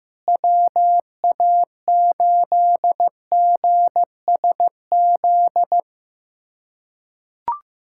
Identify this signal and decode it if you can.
Morse code, heard as a keyed tone, about 15 wpm, 700 Hz. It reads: WA8GSZ
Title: USA Callsign DE Callsign 15wpm